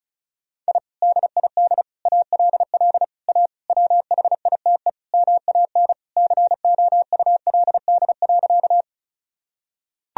Morse code, heard as a keyed tone, about 36 wpm, 700 Hz. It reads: I DID ALL A WHITE MAN COULD.
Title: Sentences from Top 1000 Words - Review Part 1 - 35wpm